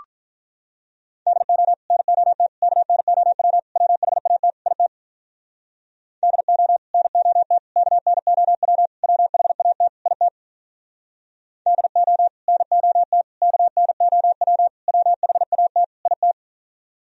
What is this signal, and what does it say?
Morse code, heard as a keyed tone, about 47 wpm, 700 Hz. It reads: DO NOT KNOW WHAT IT DO NOT KNOW WHAT IT DO NOT KNOW WHAT IT